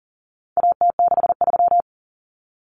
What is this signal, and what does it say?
Morse code, heard as a keyed tone, about 40 wpm, 700 Hz. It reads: AT63
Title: Common Cut Numbers - 4-Digit 40wpm